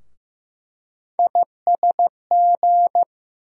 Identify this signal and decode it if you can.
Morse code, heard as a keyed tone, about 15 wpm, 700 Hz. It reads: ISG
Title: USA Call Sign Postfixes - 3 Letters 15wpm